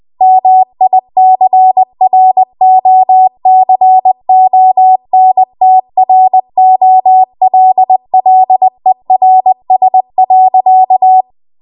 Morse code, heard as a keyed tone, about 20 wpm, 750 Hz. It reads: MICROCONTROLLERS.